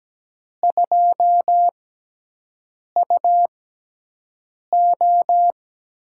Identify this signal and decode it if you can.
Morse code, heard as a keyed tone, about 17 wpm, spaced as for 6 wpm, 700 Hz. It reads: 2UO
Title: USA Call Signs - 6x Character Spacing - 17wpm